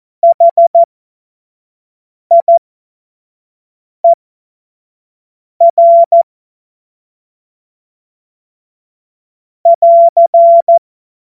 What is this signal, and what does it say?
Morse code, heard as a keyed tone, about 14 wpm, spaced as for 5 wpm, 680 Hz. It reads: HIER <AR>
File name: Morsekurs / Witze / Witz 05. Oktober 2021, 25 F70